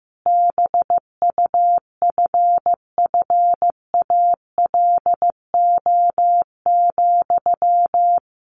Morse code, heard as a keyed tone, about 15 wpm, 700 Hz. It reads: BUFFALO,
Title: A Sound Of Thunder 15wpm